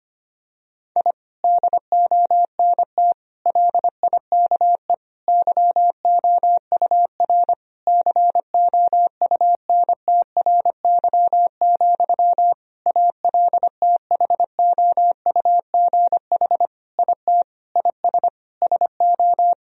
Morse code, heard as a keyed tone, about 25 wpm, 700 Hz. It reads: I DONT LIKE YOUR COUNTRY, ALTHOUGH IT IS SO
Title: Sentences from Top 1000 Words - Review Part 3 - 25wpm